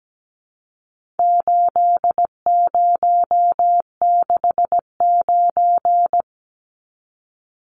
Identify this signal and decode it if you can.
Morse code, heard as a keyed tone, about 17 wpm, 700 Hz. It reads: 8069
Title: Numbers - 4-Digit 17wpm